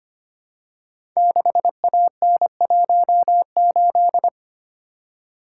Morse code, heard as a keyed tone, about 25 wpm, 700 Hz. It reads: 6AN18
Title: Code Groups - 5 Letter:Numbers - 25wpm